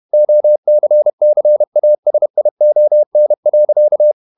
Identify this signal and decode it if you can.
Morse code, heard as a keyed tone, about 31 wpm, 600 Hz. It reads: OCCASION.
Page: Never 31wpm | CW med Gnister